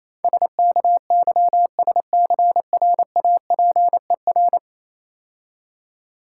Morse code, heard as a keyed tone, about 27 wpm, 700 Hz. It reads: SKYSCRAPER
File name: Easy Words - Speed-Racing with Hints - 22wpm